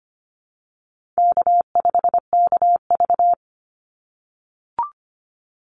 Morse code, heard as a keyed tone, about 25 wpm, 700 Hz. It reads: K5KV
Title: Easy Call Signs - 25wpm